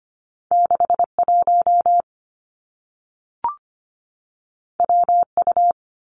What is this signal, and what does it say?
Morse code, heard as a keyed tone, about 25 wpm, 700 Hz. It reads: 61 WU
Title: Two Challenging Letter-Number Combinations 25wpm